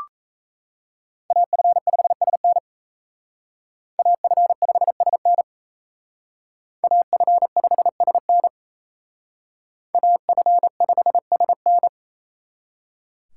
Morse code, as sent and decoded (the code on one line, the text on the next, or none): .- ..-. ..... ... -. / .- ..-. ..... ... -. / .- ..-. ..... ... -. / .- ..-. ..... ... -.
AF5SN AF5SN AF5SN AF5SN